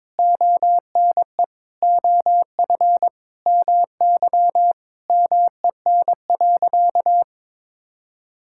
Morse code, heard as a keyed tone, about 22 wpm, 700 Hz. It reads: ONE OF MY MEN.
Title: Sentences from Top 1000 Words - Review Part 6 - 22wpm